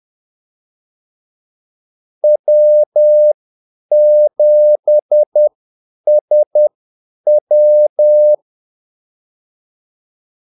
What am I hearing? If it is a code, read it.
Morse code, heard as a keyed tone, about 10 wpm, spaced as for 8 wpm, 600 Hz. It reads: W7SW
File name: Callsigns-10-8wpm